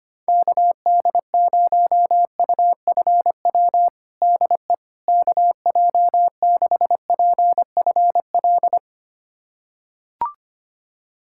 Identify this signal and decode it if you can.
Morse code, heard as a keyed tone, about 25 wpm, 700 Hz. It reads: KD0UFW DE KJ6PFL
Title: USA Callsign DE Callsign - 6 Characters - 25wpm